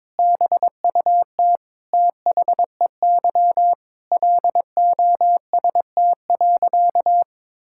Morse code, heard as a keyed tone, about 22 wpm, 700 Hz. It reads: BUT THEY LOST.
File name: practice / 22wpm / Sentences with 2-Word Phrases - 22wpm